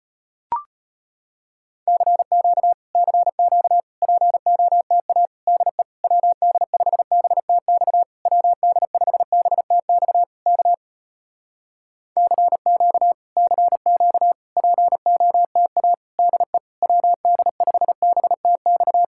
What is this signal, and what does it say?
Morse code, heard as a keyed tone, about 35 wpm, 700 Hz. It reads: CQ CQ POTA DE WD5BTX WD5BTX K CQ CQ POTA DE WD5BTX